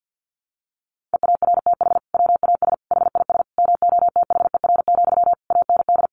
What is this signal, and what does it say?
Morse code, heard as a keyed tone, about 50 wpm, 700 Hz. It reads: EARTH WAS HIS MOTHER, AND
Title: Sentences from Top 1200 Words 50wpm